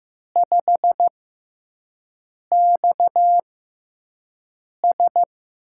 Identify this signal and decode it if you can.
Morse code, heard as a keyed tone, about 15 wpm, spaced as for 5 wpm, 700 Hz. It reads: 5XS
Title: USA Call Signs - 4 Characters - 6x Character Spacing - 15wpm